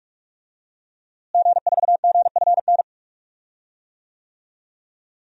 Morse code, heard as a keyed tone, about 46 wpm, 700 Hz. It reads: G4GFN
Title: DXCC-100 Call Signs - 5 Characters - Repeated 3x - 45wpm